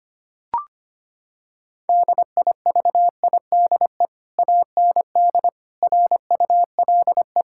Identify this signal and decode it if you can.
Morse code, heard as a keyed tone, about 25 wpm, 700 Hz. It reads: DIVIDE AND RULE
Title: Sentences from Top 1600 Words 25wpm